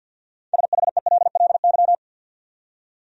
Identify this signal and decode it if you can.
Morse code, heard as a keyed tone, about 50 wpm, 700 Hz. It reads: SHELLY
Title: USA Common Names - 50wpm